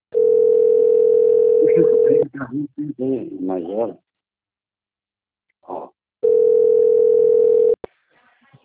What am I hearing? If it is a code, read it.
Ringback tone (North America), 440 + 480 Hz, with speech.
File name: rg-1600-unknown- - - .63087